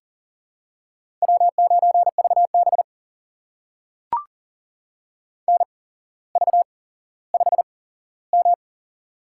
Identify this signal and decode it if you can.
Morse code, heard as a keyed tone, about 38 wpm, spaced as for 17 wpm, 700 Hz. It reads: W9VB NV5M
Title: USA Call Signs - 4 Characters - 8x Character Spacing - 40wpm